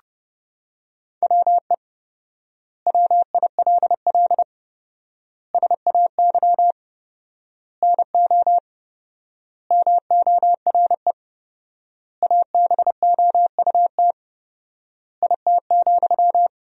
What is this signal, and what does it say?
Morse code, heard as a keyed tone, about 30 wpm, 700 Hz. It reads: WE WILL SAY NO MORE ABOUT IT,
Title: 4x Word-Spacing - Sentences from Top 200 Words - 30wpm